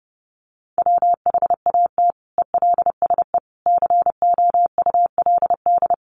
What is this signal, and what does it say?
Morse code, heard as a keyed tone, about 30 wpm, 700 Hz. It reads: WHAT ELSE COULD
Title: Sentences from Top 800 Words 30wpm